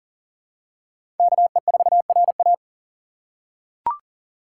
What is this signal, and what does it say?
Morse code, heard as a keyed tone, about 40 wpm, 700 Hz. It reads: KE4RA
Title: USA Call Signs - 5 Characters - 2x Character Spacing - 40wpm